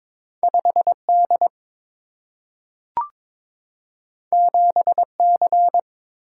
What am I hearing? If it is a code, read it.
Morse code, heard as a keyed tone, about 22 wpm, 700 Hz. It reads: 5 D 7 C
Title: Single-Letter Single-Number 22wpm